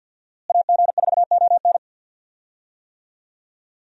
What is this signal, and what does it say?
Morse code, heard as a keyed tone, about 50 wpm, 700 Hz. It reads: AG4ON
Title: USA Call Signs - 5 Characters - Repeated 2x - 50wpm